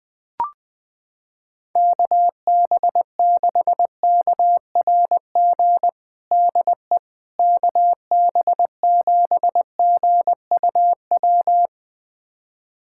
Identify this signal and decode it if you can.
Morse code, heard as a keyed tone, about 20 wpm, 700 Hz. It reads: KB6KRG DE KB7GUW
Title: USA Callsign DE Callsign - 6 Characters - 20wpm